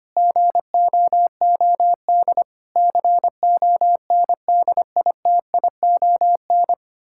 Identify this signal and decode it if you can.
Morse code, heard as a keyed tone, about 25 wpm, 700 Hz. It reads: GOOD CONDITION
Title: Pangram Challenge v2 - 25wpm